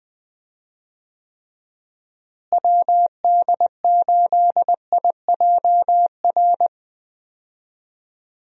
Morse code, heard as a keyed tone, about 20 wpm, 700 Hz. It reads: WD8IJR
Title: USA Call Signs - 6 Characters - Repeated 3x - 20wpm